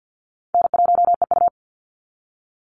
Morse code, heard as a keyed tone, about 50 wpm, 700 Hz. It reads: N1EU
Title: NA SOTA Chaser Callsigns 50wpm